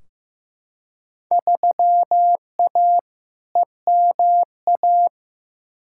Morse code, heard as a keyed tone, about 15 wpm, 700 Hz. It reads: 3A EMA
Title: Field Day Exchanges - 15wpm